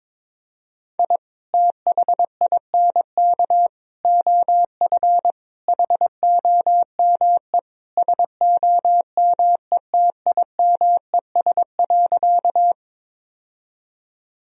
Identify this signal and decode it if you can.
Morse code, heard as a keyed tone, about 22 wpm, 700 Hz. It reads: I THINK OF HOME SOMETIMES.
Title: Sentences from Top 1000 Words - Review Part 3 - 22wpm